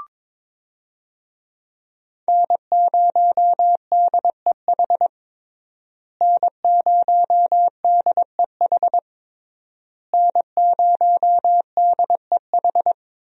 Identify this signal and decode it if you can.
Morse code, heard as a keyed tone, about 22 wpm, 700 Hz. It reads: N0DEH N0DEH N0DEH
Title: USA Call Signs - 5 Characters - Repeated 3x - 22wpm